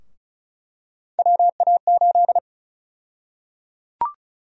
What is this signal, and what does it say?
Morse code, heard as a keyed tone, about 34 wpm, 700 Hz. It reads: WA8
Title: USA Call Sign Prefixes - 35wpm